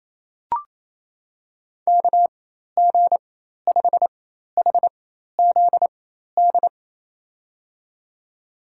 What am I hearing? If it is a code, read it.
Morse code, heard as a keyed tone, about 28 wpm, spaced as for 13 wpm, 700 Hz. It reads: KG5HZD
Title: USA Call Signs - 4x Character Spacing - 28wpm